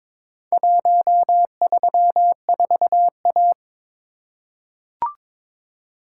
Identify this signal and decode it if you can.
Morse code, heard as a keyed tone, about 22 wpm, 700 Hz. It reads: 134A
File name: practice / 22wpm / Four Letter-Number Combinations - 22wpm